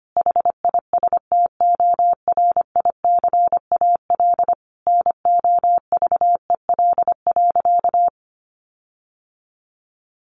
Morse code, heard as a keyed tone, about 25 wpm, 700 Hz. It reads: HISTORICAL NOVEL.
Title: Sentences from Top 1300 Words 25wpm